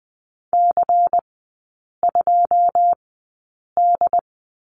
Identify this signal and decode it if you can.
Morse code, heard as a keyed tone, about 20 wpm, spaced as for 8 wpm, 700 Hz. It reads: C2D
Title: 166 Lesson 27e - Number 7 - Three Characters 20wpm